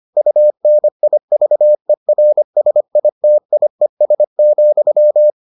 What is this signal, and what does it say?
Morse code, heard as a keyed tone, about 25 wpm, 600 Hz. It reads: UNIVERSITIES,